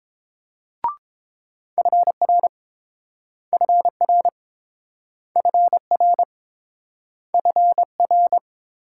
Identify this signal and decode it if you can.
Morse code, heard as a keyed tone, about 30 wpm, 700 Hz. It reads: FR FR FR FR